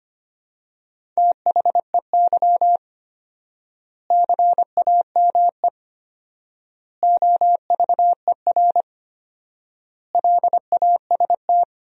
Morse code, heard as a keyed tone, about 25 wpm, 700 Hz. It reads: THEY CAME OVER LAST
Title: 4x Word-Spacing - Sentences from Top 300 Words - 25wpm